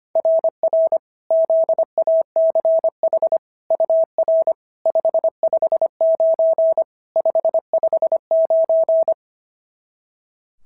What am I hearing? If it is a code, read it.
Morse code, heard as a keyed tone, about 25 wpm, 690 Hz. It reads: RR ZACH UR 559 559
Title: SOTA QSO Practice 25wpm